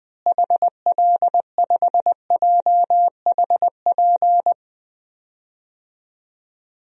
Morse code, heard as a keyed tone, about 20 wpm, 700 Hz. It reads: HL5JHP
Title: DXCC-100 Call Signs - 6 Characters - Repeated 2x - 20wpm